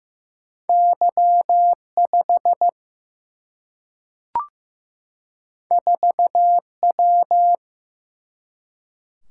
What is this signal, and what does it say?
Morse code, heard as a keyed tone, about 15 wpm, 700 Hz. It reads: Y 5 4 W
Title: Single-Letter Single-Number 15wpm